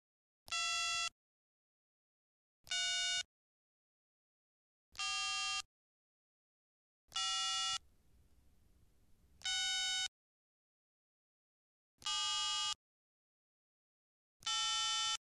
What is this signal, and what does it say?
DTMF, heard as telephone touch tones, tones of about 620 ms. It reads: 2 3 4 5 6 7 8
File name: Tone dialing sounds on the handset